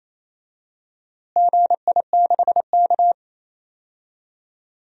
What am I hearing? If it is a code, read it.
Morse code, heard as a keyed tone, about 28 wpm, 700 Hz. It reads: GI6K
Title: DXCC-100 Call Signs - 4 Characters - Repeated 2x - 28wpm